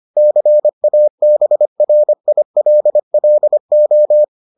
Morse code, heard as a keyed tone, about 25 wpm, 600 Hz. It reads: CABRILLO